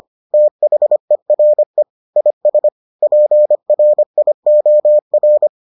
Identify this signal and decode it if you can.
Morse code, heard as a keyed tone, about 25 wpm, 600 Hz. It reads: THERE IS PRIOR